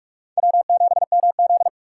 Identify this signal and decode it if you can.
Morse code, heard as a keyed tone, about 41 wpm, 700 Hz. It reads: W7MZ